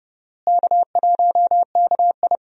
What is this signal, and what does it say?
Morse code, heard as a keyed tone, about 30 wpm, 700 Hz. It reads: K1KI
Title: USA Call Signs - 4 Characters - Repeated 2x - 30wpm